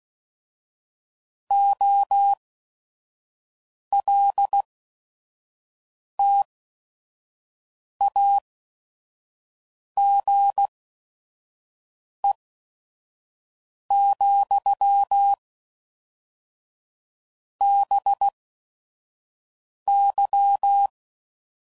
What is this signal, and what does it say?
Morse code, heard as a keyed tone, about 16 wpm, spaced as for 5 wpm, 800 Hz. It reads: OLTAGE, BY
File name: CW Code Groups @5WPM / Technical Subjects / XC